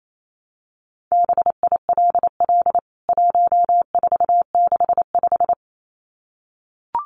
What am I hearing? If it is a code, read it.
Morse code, heard as a keyed tone, about 28 wpm, 700 Hz. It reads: BILL 1465
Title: CWT Exchange 28wpm